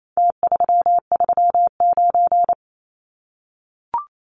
Morse code, heard as a keyed tone, about 28 wpm, 700 Hz. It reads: T339
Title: Numbers - Contest Style - 4-Digit 28wpm